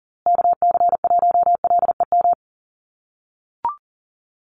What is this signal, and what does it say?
Morse code, heard as a keyed tone, about 37 wpm, 700 Hz. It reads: KC1LEM
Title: USA Call Signs Repeated 3x 40wpm